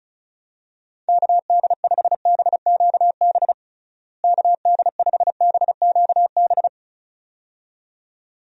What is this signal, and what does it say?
Morse code, heard as a keyed tone, about 34 wpm, 700 Hz. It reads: KD5BQB KD5BQB
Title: USA Call Signs - 6 Characters - Repeated 2x - 35wpm